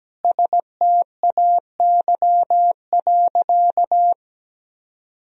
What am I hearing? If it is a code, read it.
Morse code, heard as a keyed tone, about 17 wpm, 700 Hz. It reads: STAY.
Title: Sentences from Top 1000 Words - Review Part 3 - 17wpm